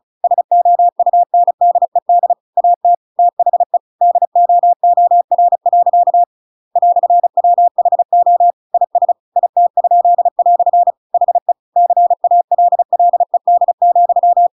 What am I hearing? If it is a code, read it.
Morse code, heard as a keyed tone, about 35 wpm, 700 Hz. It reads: SOUNDED AT THE DOOR. "WHO IS IT?" HE CALLED,